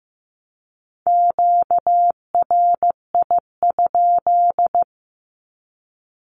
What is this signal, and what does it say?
Morse code, heard as a keyed tone, about 15 wpm, 700 Hz. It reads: QRI?
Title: Q-Codes Repeated 3x 15wpm